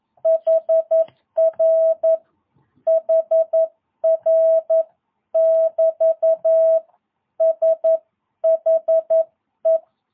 Morse code, heard as a keyed tone, about 11 wpm, 660 Hz. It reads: HR HR <BT> SHE
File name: jh8cbh / 2022cw